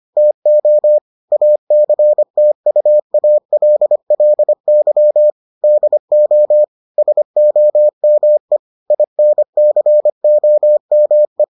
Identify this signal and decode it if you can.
Morse code, heard as a keyed tone, about 25 wpm, 600 Hz. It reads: TO ACTUALLY DO SOME INCOME